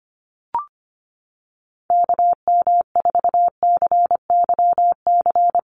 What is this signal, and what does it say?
Morse code, heard as a keyed tone, about 25 wpm, 700 Hz. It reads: KM4CYC
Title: USA Call Signs Repeated 3x 25wpm